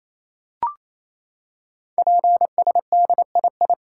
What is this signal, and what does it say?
Morse code, heard as a keyed tone, about 28 wpm, 700 Hz. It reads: PSDII
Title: Code Groups - 5 Letter:Numbers - 28wpm